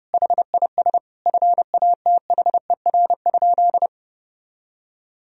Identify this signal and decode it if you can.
Morse code, heard as a keyed tone, about 30 wpm, 700 Hz. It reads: HIS FATHER?
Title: Sentences from Top 1000 Words - Review Part 2 - 30wpm